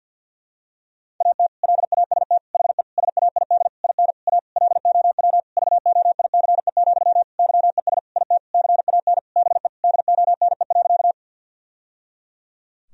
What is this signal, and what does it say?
Morse code, heard as a keyed tone, about 53 wpm, 700 Hz. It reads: AT LAST HE SAID IN A LOW VOICE, YES IT CAN BE DONE.
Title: Sentences from Top 1000 Words - Review Part 1 - 50wpm